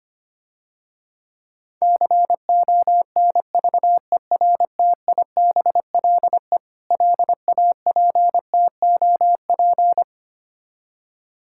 Morse code, heard as a keyed tone, about 25 wpm, 700 Hz. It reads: CONVERTIBLE LAPTOP
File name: practice / 25wpm / Difficult Words - Speed-Racing with Hints - 25wpm